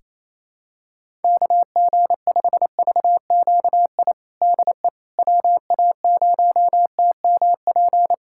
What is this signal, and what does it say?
Morse code, heard as a keyed tone, about 28 wpm, 700 Hz. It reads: KG5VQI DE WA0TMP
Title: USA Callsign DE Callsign 28wpm